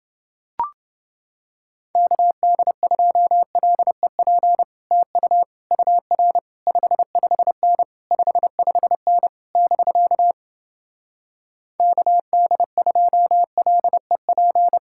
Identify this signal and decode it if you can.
Morse code, heard as a keyed tone, about 29 wpm, 700 Hz. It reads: KD2LEP TU UR 55N 55N <BK> KD2LEP